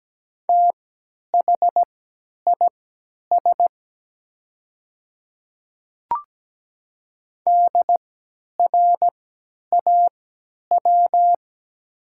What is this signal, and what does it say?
Morse code, heard as a keyed tone, about 17 wpm, spaced as for 10 wpm, 690 Hz. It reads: THIS DRAW
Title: 3x Character Spacing - Top 300 Words - 17wpm